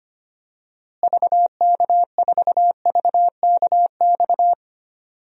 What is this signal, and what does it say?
Morse code, heard as a keyed tone, about 25 wpm, 700 Hz. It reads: VK4VKX
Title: DXCC-100 Call Signs - 6 Characters - Repeated 2x - 25wpm